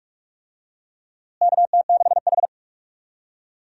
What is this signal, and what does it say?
Morse code, heard as a keyed tone, about 41 wpm, 700 Hz. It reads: KT6H
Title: USA Call Signs - 4 Characters - Repeated 3x - 45wpm